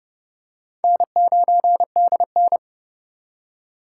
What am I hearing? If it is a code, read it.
Morse code, heard as a keyed tone, about 32 wpm, 700 Hz. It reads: N9DN
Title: USA Call Signs - 4 Characters - 3x Character Spacing - 30wpm